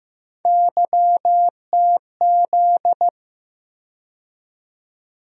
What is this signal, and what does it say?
Morse code, heard as a keyed tone, about 15 wpm, 700 Hz. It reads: YTZ
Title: USA Call Sign Postfixes - 3 Letters 15wpm